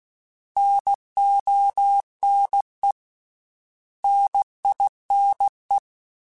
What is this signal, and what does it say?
Morse code, heard as a keyed tone, about 16 wpm, 800 Hz. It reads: NONE NINE
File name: Misc / Misc-mp3s / Misc @15